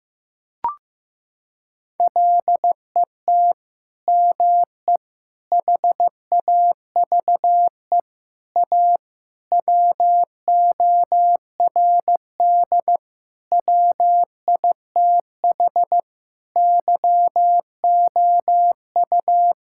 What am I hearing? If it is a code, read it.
Morse code, heard as a keyed tone, about 14 wpm, 700 Hz. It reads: LET ME HAVE A WORD WITH YOU
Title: Sentences from Top 1000 Words - Review Part 6 - 15wpm